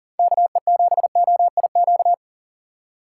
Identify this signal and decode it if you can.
Morse code, heard as a keyed tone, about 40 wpm, 690 Hz. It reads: KE7OIQ
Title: USA Call Signs - 6 Characters - 2x Character Spacing - 40wpm